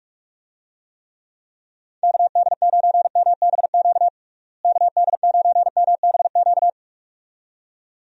Morse code, heard as a keyed tone, about 46 wpm, 700 Hz. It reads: KD9MBQ KD9MBQ
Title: USA Call Signs - 6 Characters - Repeated 2x - 45wpm